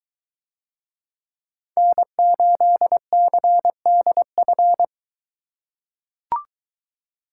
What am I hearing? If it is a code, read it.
Morse code, heard as a keyed tone, about 22 wpm, 700 Hz. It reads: N8CDF